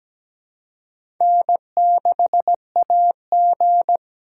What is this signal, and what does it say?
Morse code, heard as a keyed tone, about 17 wpm, 700 Hz. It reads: N6AG
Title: USA Call Signs - 4 Characters - 3x Character Spacing - 17wpm